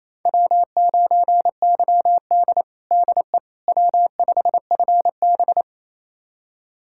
Morse code, heard as a keyed tone, about 28 wpm, 700 Hz. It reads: W9YD DE W5FB
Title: USA Callsign DE Callsign - 4 Characters - 28wpm